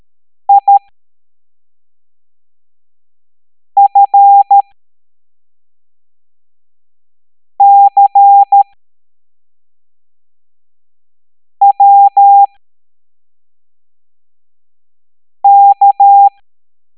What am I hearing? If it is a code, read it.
Morse code, heard as a keyed tone, about 13 wpm, spaced as for 3 wpm, 800 Hz. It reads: IFCWK